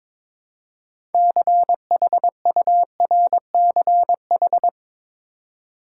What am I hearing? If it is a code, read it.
Morse code, heard as a keyed tone, about 22 wpm, 700 Hz. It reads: CHURCH
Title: Six Letter Words - 22wpm